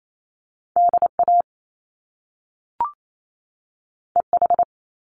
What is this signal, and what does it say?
Morse code, heard as a keyed tone, about 28 wpm, 700 Hz. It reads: DA EH
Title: Top Two Letter Combinations in Words 28wpm